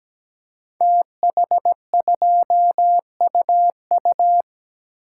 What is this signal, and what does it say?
Morse code, heard as a keyed tone, about 17 wpm, 700 Hz. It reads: TH2UU
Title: Code Groups - 5 Letter:Numbers - 17wpm